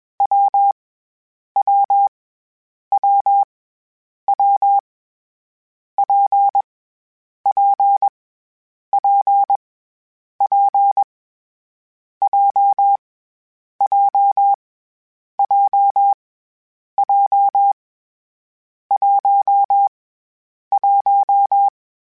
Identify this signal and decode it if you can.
Morse code, heard as a keyed tone, about 21 wpm, 800 Hz. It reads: W W W W P P P P J J J J 1 1
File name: Difficult Combos / Difficult Combos-mp3s / WPJ WPJ1 @20